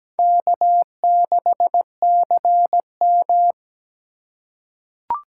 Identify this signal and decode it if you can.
Morse code, heard as a keyed tone, about 17 wpm, 700 Hz. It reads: K6CM
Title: USA Call Signs - 4x Character Spacing - 17wpm